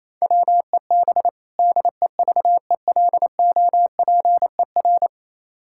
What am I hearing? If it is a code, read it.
Morse code, heard as a keyed tone, about 28 wpm, 700 Hz. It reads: WEB DEVELOPER
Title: Practice QSO - Exchange 3 - More Info - 28wpm